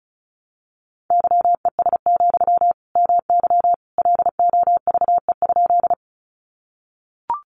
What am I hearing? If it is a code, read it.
Morse code, heard as a keyed tone, about 34 wpm, 700 Hz. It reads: YES, MY LOVE?
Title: Sentences from Top 500 Words 35wpm